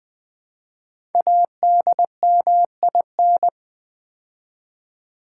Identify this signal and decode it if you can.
Morse code, heard as a keyed tone, about 20 wpm, 700 Hz. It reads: ADMIN
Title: Build A Word - 5 Letters At A Time - 10-Letter Words - 20wpm